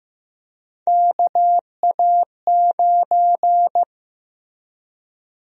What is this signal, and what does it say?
Morse code, heard as a keyed tone, about 15 wpm, 700 Hz. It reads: KA9
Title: USA Call Sign Prefixes - Repeated 2x - 15wpm